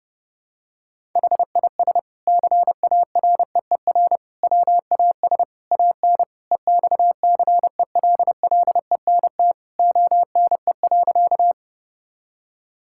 Morse code, heard as a keyed tone, about 30 wpm, 700 Hz. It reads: HIS CAREER WAS AN EXCELLENT ONE.
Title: Sentences from Top 1000 Words - Review Part 6 - 30wpm